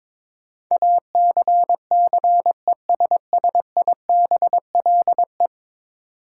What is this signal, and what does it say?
Morse code, heard as a keyed tone, about 22 wpm, 700 Hz. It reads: ACCESSIBLE
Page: morse-code-ninja-practice-22wpm